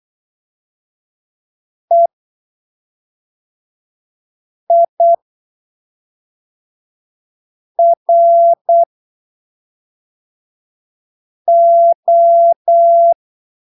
Morse code, heard as a keyed tone, about 8 wpm, spaced as for 3 wpm, 680 Hz. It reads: EIRO
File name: arquivos / texto / texto 29 08ppm